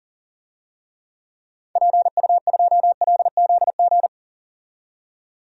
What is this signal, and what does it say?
Morse code, heard as a keyed tone, about 37 wpm, 700 Hz. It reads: PU2LZG
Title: DXCC-100 Call Signs - 6 Characters - 40wpm